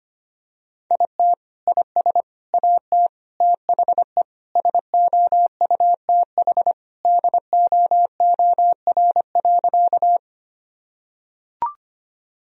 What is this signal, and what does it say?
Morse code, heard as a keyed tone, about 25 wpm, 700 Hz. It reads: IT IS AT THE SOUTH DOOR.
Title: Sentences from Top 1000 Words - Review Part 2 - 25wpm